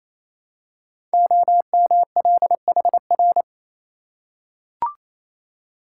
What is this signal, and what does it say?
Morse code, heard as a keyed tone, about 28 wpm, 700 Hz. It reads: OMLHR
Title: Code Groups - 5 Letters - 28wpm